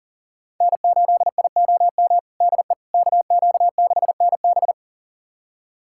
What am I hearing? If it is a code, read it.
Morse code, heard as a keyed tone, about 40 wpm, 700 Hz. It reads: N8IOM DE KQ6NB
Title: USA Callsign DE Callsign - 5 Characters - 40wpm